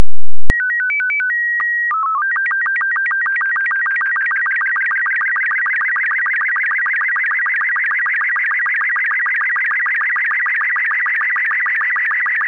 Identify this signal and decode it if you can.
SSTV (slow-scan television), Robot 36.
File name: robot36